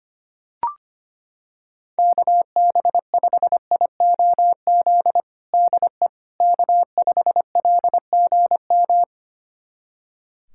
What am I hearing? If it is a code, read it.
Morse code, heard as a keyed tone, about 25 wpm, 700 Hz. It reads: KB5IOZ DE K5LGM
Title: USA Callsign DE Callsign 25wpm